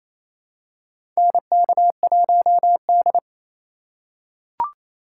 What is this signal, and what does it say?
Morse code, heard as a keyed tone, about 28 wpm, 700 Hz. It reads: NK1D
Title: USA Call Signs - 4 Characters - 3x Character Spacing - 28wpm